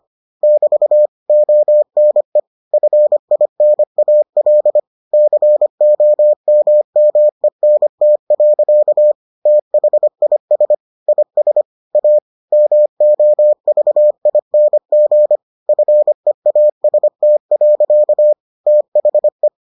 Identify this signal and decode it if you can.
Morse code, heard as a keyed tone, about 25 wpm, 600 Hz. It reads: <BT> ONE FINAL COMMENT. THIS IS A MOVING FEAST. THE